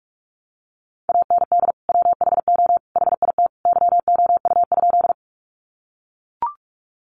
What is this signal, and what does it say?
Morse code, heard as a keyed tone, about 44 wpm, 700 Hz. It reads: AND WHO HIT YOU?
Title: Sentences from Top 1200 Words 45wpm